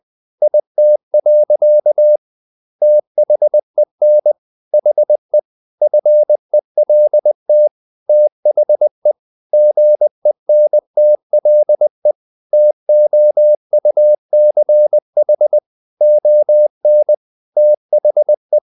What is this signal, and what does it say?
Morse code, heard as a keyed tone, about 20 wpm, 600 Hz. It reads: IT. THEN HE FELT THE GENTLE TOUCH ON THE
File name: the old man and the sea / the old man and the sea part 1 20wpm